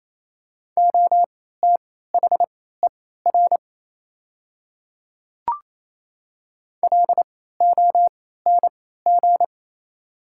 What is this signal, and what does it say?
Morse code, heard as a keyed tone, about 28 wpm, spaced as for 16 wpm, 690 Hz. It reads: OTHER LONG
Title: 3x Character Spacing - Top 500 Words - 28wpm